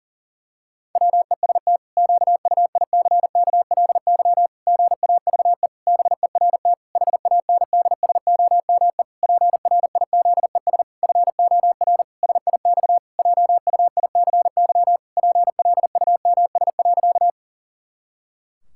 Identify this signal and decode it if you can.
Morse code, heard as a keyed tone, about 40 wpm, 700 Hz. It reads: WEST QUICKLY GAVE BERT HANDSOME PRIZES FOR SIX JUICY PLUMS.
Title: Pangram Challenge v2 - 40wpm